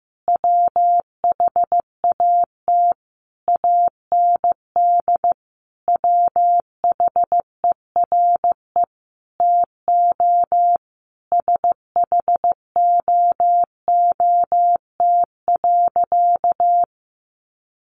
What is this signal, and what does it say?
Morse code, heard as a keyed tone, about 15 wpm, 700 Hz. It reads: WHAT AND WHERE TO SHOOT.
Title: A Sound Of Thunder 15wpm